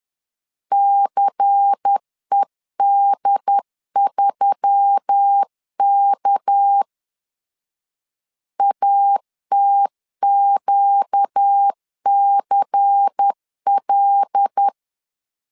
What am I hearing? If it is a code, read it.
Morse code, heard as a keyed tone, about 10 wpm, 800 Hz. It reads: CED3K ATQCL
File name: CW / DET / 1 6 1500C S C 10wpm 4252